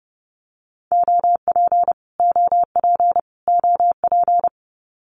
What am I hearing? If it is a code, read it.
Morse code, heard as a keyed tone, about 30 wpm, 700 Hz. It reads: OP OP OP
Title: Abbreviations Repeated 3x - 30wpm